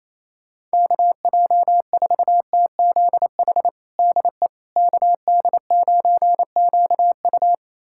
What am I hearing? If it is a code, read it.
Morse code, heard as a keyed tone, about 28 wpm, 700 Hz. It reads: KJ4TZH DE KD9QU
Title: USA Callsign DE Callsign 28wpm